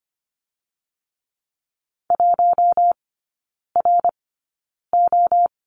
Morse code, heard as a keyed tone, about 25 wpm, spaced as for 9 wpm, 690 Hz. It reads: 1RO
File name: Lesson 15e - Number 5 - Three Characters 25wpm